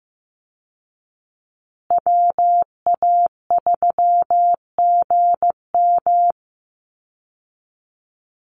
Morse code, heard as a keyed tone, about 15 wpm, 700 Hz. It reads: WA3GM
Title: Call Signs Repeated 2x - 15wpm